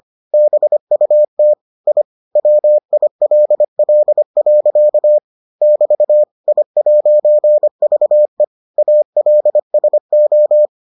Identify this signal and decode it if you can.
Morse code, heard as a keyed tone, about 25 wpm, 600 Hz. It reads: BUT I WILL. <BT> I'VE ALSO